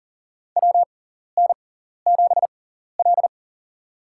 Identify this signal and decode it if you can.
Morse code, heard as a keyed tone, about 40 wpm, spaced as for 14 wpm, 700 Hz. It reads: WN7L
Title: USA Call Signs - 4 Characters - 6x Character Spacing - 40wpm